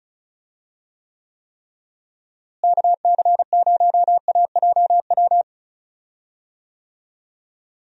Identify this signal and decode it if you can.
Morse code, heard as a keyed tone, about 34 wpm, 700 Hz. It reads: KC0AJW
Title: USA Call Signs - 6 Characters - Repeated 3x - 35wpm